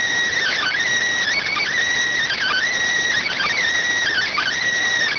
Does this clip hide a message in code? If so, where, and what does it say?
Yes: Emergency Alert System (SAME) tones from 0.4 to 4.6 s, heard as harsh buzzing bursts.